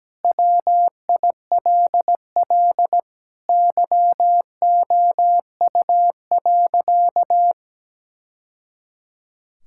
Morse code, heard as a keyed tone, about 17 wpm, 700 Hz. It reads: WILL YOU.
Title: Sentences from Top 1000 Words - Review Part 6 - 17wpm